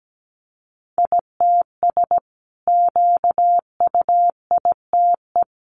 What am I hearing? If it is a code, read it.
Morse code, heard as a keyed tone, about 17 wpm, 700 Hz. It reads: ITS QUITE
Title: Sentences from Top 700 Words 17wpm